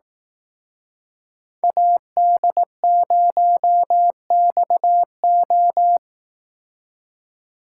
Morse code, heard as a keyed tone, about 18 wpm, 700 Hz. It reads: AD0XO
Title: Speed Racing - USA 2x2 Call Signs 15wpm